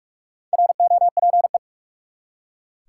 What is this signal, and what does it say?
Morse code, heard as a keyed tone, about 45 wpm, 700 Hz. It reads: ROPE
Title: Four Letter Words - 45wpm